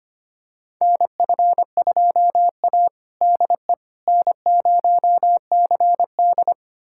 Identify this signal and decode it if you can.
Morse code, heard as a keyed tone, about 25 wpm, 700 Hz. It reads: NF2A DE N0CD
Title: USA Callsign DE Callsign - 4 Characters - 25wpm